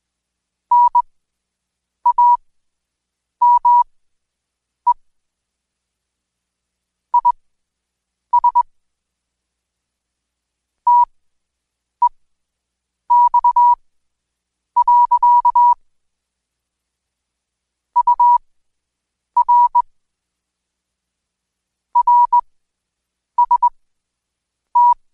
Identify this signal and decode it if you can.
Morse code, heard as a keyed tone, about 20 wpm, spaced as for 7 wpm, 970 Hz. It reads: NAME IS TEX. UR RST
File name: Code Quick / AudioDownld2 / Seg112-Day17-DayDrill 17